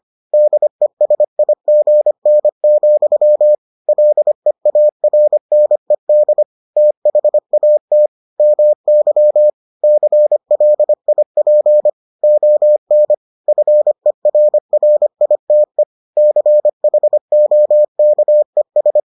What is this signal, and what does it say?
Morse code, heard as a keyed tone, about 25 wpm, 600 Hz. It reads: DESIGN, LEARNED THAT MY CLIP ON FERRITE CHOKES